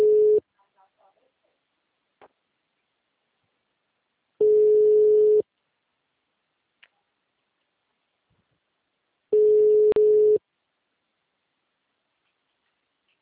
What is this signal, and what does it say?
Ringback tone (Europe, 425 Hz).